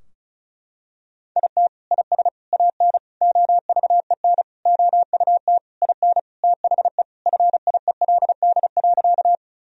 Morse code, heard as a keyed tone, about 36 wpm, 700 Hz. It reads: IT IS AN OVEN OUT IN THE FIELD.
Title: Sentences from Top 1000 Words - Review Part 4 - 35wpm